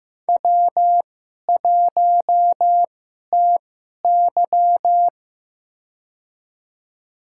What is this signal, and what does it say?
Morse code, heard as a keyed tone, about 15 wpm, spaced as for 11 wpm, 700 Hz. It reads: W1TY
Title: USA Call Signs - 2x Character Spacing - 15wpm